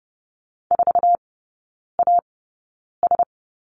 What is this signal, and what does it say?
Morse code, heard as a keyed tone, about 30 wpm, spaced as for 9 wpm, 700 Hz. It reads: 4AS
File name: Lesson 18e - Letter m - Three Characters 30wpm